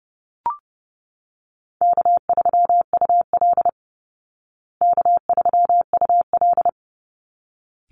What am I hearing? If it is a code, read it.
Morse code, heard as a keyed tone, about 30 wpm, 700 Hz. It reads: K3UL K3UL
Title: Easy Call Signs Repeated 2x - 30wpm